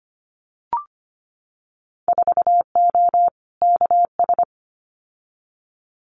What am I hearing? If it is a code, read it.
Morse code, heard as a keyed tone, about 25 wpm, 700 Hz. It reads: 4O KS
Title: Winter Field Day Exchanges Encore 25wpm